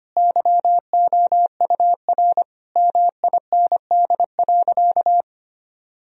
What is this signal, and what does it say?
Morse code, heard as a keyed tone, about 25 wpm, 700 Hz. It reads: YOUR MIND.
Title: Sentences from Top 1000 Words - Review Part 3 - 25wpm